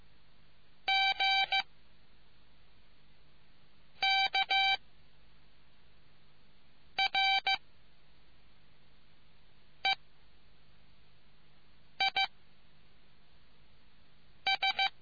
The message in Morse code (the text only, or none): G K R E I S